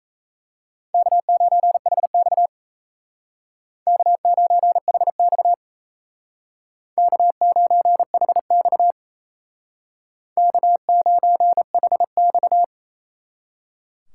Morse code, heard as a keyed tone, about 33 wpm, 700 Hz. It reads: K9HX K9HX K9HX K9HX
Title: Speed Racing - USA 1x2 Call Signs 28wpm